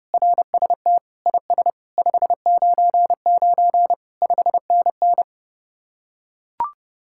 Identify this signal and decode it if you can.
Morse code, heard as a keyed tone, about 32 wpm, 700 Hz. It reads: RST IS 599 5NN